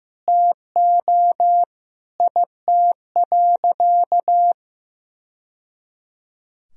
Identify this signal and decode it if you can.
Morse code, heard as a keyed tone, about 15 wpm, 700 Hz. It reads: TO IT.
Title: Sentences from Top 1000 Words - Review Part 2 - 15wpm